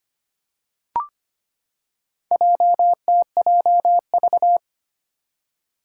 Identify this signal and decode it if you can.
Morse code, heard as a keyed tone, about 25 wpm, 700 Hz. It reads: JTJV
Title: Four Challenging Letter-Number Combinations 25wpm